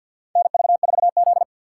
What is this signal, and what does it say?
Morse code, heard as a keyed tone, about 48 wpm, 700 Hz. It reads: NV4Z
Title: USA Call Signs - 4 Characters - 2x Character Spacing - 50wpm